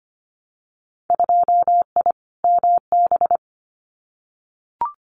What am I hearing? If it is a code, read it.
Morse code, heard as a keyed tone, about 25 wpm, 700 Hz. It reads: 2I MB
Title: Winter Field Day Exchanges Encore 25wpm